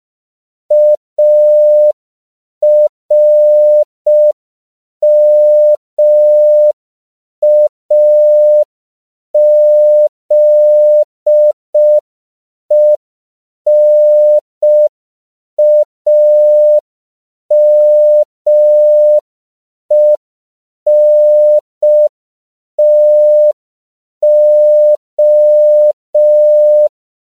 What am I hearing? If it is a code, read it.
Morse code, heard as a keyed tone, about 5 wpm, 600 Hz. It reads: ARMAZENAMENTO
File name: CW / 5PPM